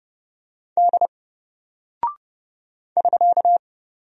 Morse code, heard as a keyed tone, about 30 wpm, 700 Hz. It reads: D <SK>
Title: Single Character (Letter:Number:Punctuation:Prosign) 30wpm